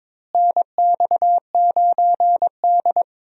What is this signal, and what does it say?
Morse code, heard as a keyed tone, about 22 wpm, 700 Hz. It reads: NX9D
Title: USA Call Signs - 4 Characters - Repeated 2x - 22wpm